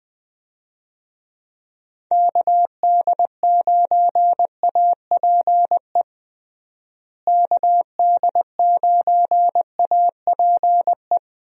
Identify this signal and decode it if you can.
Morse code, heard as a keyed tone, about 20 wpm, 700 Hz. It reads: KD9APE KD9APE
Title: USA Call Signs - 6 Characters - Repeated 2x - 20wpm